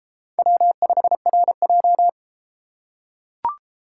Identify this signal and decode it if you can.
Morse code, heard as a keyed tone, about 33 wpm, 700 Hz. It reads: W5RJ